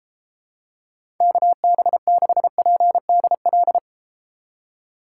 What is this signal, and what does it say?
Morse code, heard as a keyed tone, about 32 wpm, 700 Hz. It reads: KB6PDL